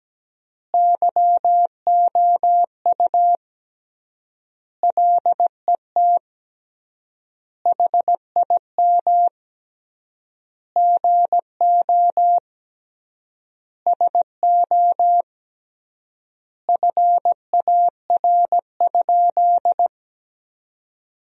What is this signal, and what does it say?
Morse code, heard as a keyed tone, about 17 wpm, 700 Hz. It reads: YOU LET HIM GO SO FAR?
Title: 3x Word-Spacing - Sentences from Top 300 Words - 17wpm